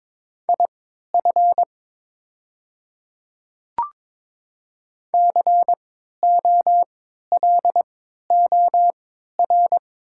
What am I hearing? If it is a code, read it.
Morse code, heard as a keyed tone, about 22 wpm, spaced as for 12 wpm, 700 Hz. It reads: IF COLOR
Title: 3x Character Spacing - Top 500 Words - 22wpm